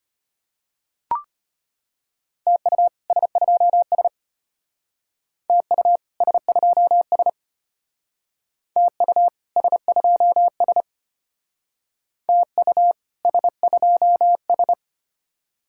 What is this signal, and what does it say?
Morse code, heard as a keyed tone, about 31 wpm, 700 Hz. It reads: TU S2S TU S2S TU S2S TU S2S
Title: Speed-Racing Common QSO Phrases - 25wpm